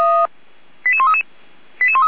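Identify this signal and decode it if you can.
Five-tone selective call (ZVEI), heard as quick runs of five tones, tones of about 70 ms. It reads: 88133
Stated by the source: ZVEI five-tone selective call received on an SDR.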